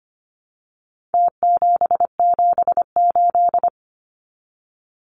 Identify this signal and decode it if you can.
Morse code, heard as a keyed tone, about 25 wpm, 700 Hz. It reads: T778
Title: Numbers - Contest Style - 4-Digit 25wpm